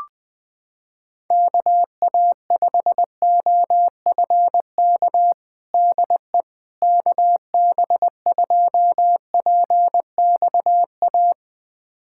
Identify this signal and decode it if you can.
Morse code, heard as a keyed tone, about 20 wpm, 700 Hz. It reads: KA5OFK DE KB2PXA
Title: USA Callsign DE Callsign - 6 Characters - 20wpm